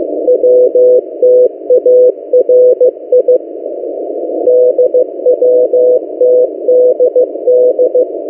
Morse code, heard as a keyed tone, about 15 wpm, 500 Hz. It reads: WTARI DWTDD
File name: M08a- -1355utc-8096khz